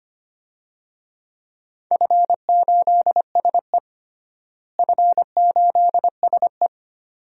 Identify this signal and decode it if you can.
Morse code, heard as a keyed tone, about 25 wpm, 700 Hz. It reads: F8SE F8SE
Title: DXCC-100 Call Signs - 4 Characters - Repeated 2x - 25wpm